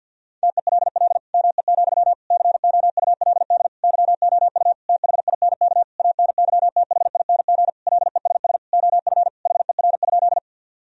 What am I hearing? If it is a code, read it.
Morse code, heard as a keyed tone, about 52 wpm, 700 Hz. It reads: TELL ME, COULD YOU THINK ANYTHING LESS OF HER?
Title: Sentences from Top 1000 Words - Review Part 2 - 50wpm